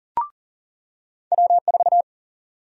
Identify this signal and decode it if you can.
Morse code, heard as a keyed tone, about 40 wpm, 700 Hz. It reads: W4
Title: USA Call Sign Prefixes - 40wpm